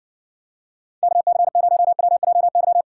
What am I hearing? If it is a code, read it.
Morse code, heard as a keyed tone, about 60 wpm, 700 Hz. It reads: KC0WJY